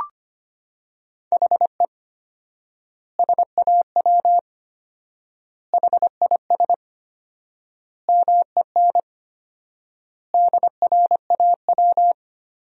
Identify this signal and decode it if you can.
Morse code, heard as a keyed tone, about 25 wpm, 700 Hz. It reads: HE SAW HIS MEN DRAW
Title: 4x Word-Spacing - Sentences from Top 300 Words - 25wpm